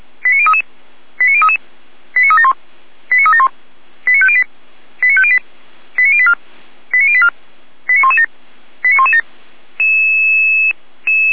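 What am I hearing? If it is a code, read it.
Five-tone selective call (ZVEI), heard as quick runs of five tones, tones of about 70 ms. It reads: 89033 89033 89371 89371 89508 89508 89084 89084 89117 89117
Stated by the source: ZVEI five-tone call received by SDR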